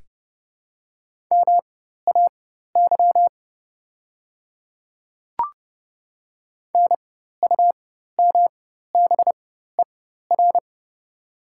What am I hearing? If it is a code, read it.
Morse code, heard as a keyed tone, about 30 wpm, spaced as for 14 wpm, 700 Hz. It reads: MAY NUMBER
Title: 4x Character Spacing - Top 100 Words - 30wpm